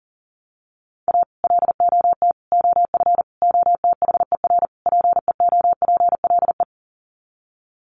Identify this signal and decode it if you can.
Morse code, heard as a keyed tone, about 40 wpm, 700 Hz. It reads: A LOT OF OTHER PEOPLE
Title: Sets of 5 Words Encore2 40wpm